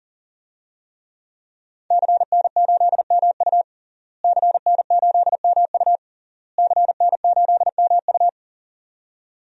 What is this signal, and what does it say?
Morse code, heard as a keyed tone, about 37 wpm, 700 Hz. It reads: CN8MU CN8MU CN8MU
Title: DXCC-100 Call Signs - 5 Characters - Repeated 3x - 40wpm